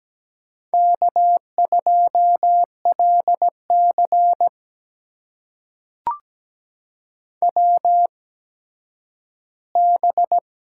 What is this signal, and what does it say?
Morse code, heard as a keyed tone, about 17 wpm, spaced as for 10 wpm, 700 Hz. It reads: K2LC WB
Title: USA Call Signs - 8x Character Spacing - 17wpm